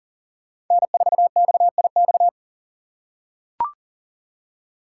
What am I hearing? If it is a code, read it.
Morse code, heard as a keyed tone, about 42 wpm, 700 Hz. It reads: N4XIX
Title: USA Call Signs - 5 Characters - 6x Character Spacing - 40wpm